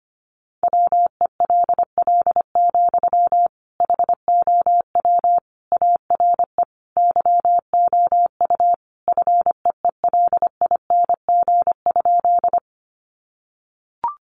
Morse code, heard as a keyed tone, about 25 wpm, 700 Hz. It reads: WELL, HOW ARE YOU FEELING?
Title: Sentences from Top 1200 Words 25wpm